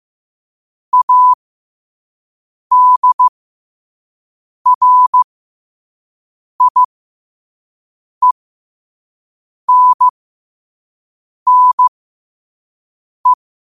Morse code, heard as a keyed tone, about 15 wpm, spaced as for 5 wpm, 1000 Hz. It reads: ADRIENNE
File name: qotdfiles / 5wpm